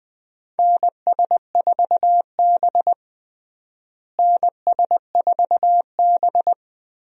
Morse code, heard as a keyed tone, about 20 wpm, 700 Hz. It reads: NS4B NS4B
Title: USA Call Signs - 4 Characters - Repeated 3x - 20wpm